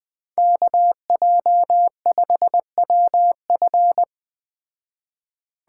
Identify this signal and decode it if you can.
Morse code, heard as a keyed tone, about 20 wpm, 700 Hz. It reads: KJ5WF
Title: Speed Racing - USA 2x2 Call Signs 17wpm